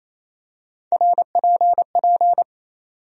Morse code, heard as a keyed tone, about 28 wpm, 700 Hz. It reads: RPP
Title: USA Call Sign Postfixes - 3 Letters 28wpm